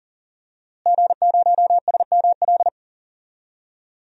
Morse code, heard as a keyed tone, about 40 wpm, 700 Hz. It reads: G0SML
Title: DXCC-100 Call Signs - 5 Characters - Repeated 2x - 40wpm